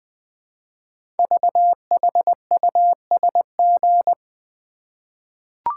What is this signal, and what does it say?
Morse code, heard as a keyed tone, about 20 wpm, 700 Hz. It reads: VHUSG
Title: Code Groups - 5 Letters - 20wpm